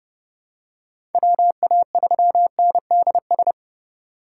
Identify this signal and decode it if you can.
Morse code, heard as a keyed tone, about 31 wpm, 700 Hz. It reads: WA3NDS
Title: USA Call Signs - 6 Characters - 4x Character Spacing - 30wpm